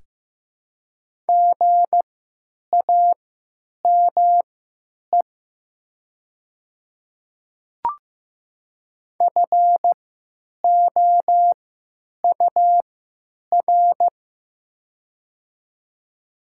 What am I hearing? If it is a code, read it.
Morse code, heard as a keyed tone, about 15 wpm, spaced as for 9 wpm, 700 Hz. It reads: GAME FOUR
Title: 3x Character Spacing - Top 500 Words - 15wpm